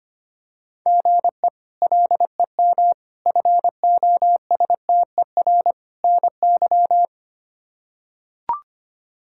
Morse code, heard as a keyed tone, about 25 wpm, 700 Hz. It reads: GE LEM FOSTER NY
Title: K1USN SST QSO Practice - 25wpm